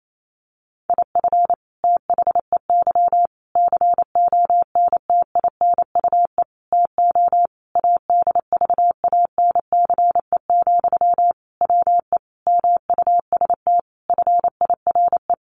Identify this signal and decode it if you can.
Morse code, heard as a keyed tone, about 28 wpm, 700 Hz. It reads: IF THEY CONTINUE TO ADVANCE, WE MUST FIRE
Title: Sentences from Top 1400 Words 28wpm